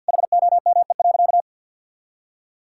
Morse code, heard as a keyed tone, about 52 wpm, 700 Hz. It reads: HOME.
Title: Sentences from Top 1000 Words - Review Part 5 - 50wpm